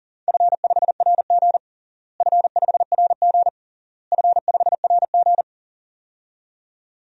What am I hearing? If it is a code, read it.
Morse code, heard as a keyed tone, about 40 wpm, 700 Hz. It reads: F5RG F5RG F5RG
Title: DXCC-100 Call Signs - 4 Characters - Repeated 3x - 40wpm